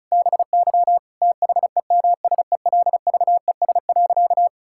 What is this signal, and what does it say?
Morse code, heard as a keyed tone, about 36 wpm, 700 Hz. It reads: BY THEMSELVES.
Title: Sentences from Top 1000 Words - Review Part 5 - 35wpm